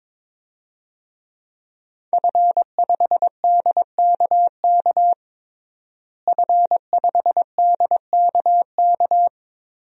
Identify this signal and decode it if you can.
Morse code, heard as a keyed tone, about 21 wpm, 700 Hz. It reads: F5DKK F5DKK
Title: DXCC-100 Call Signs - 5 Characters - Repeated 3x - 22wpm